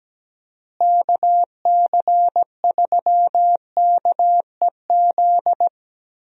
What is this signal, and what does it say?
Morse code, heard as a keyed tone, about 17 wpm, 700 Hz. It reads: KC3KEZ
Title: USA Call Signs - 6 Characters - Repeated 3x - 17wpm